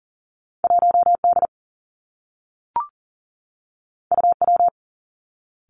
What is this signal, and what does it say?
Morse code, heard as a keyed tone, about 42 wpm, 700 Hz. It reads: 1 D U W
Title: Two Challenging Letter-Number Combinations 40wpm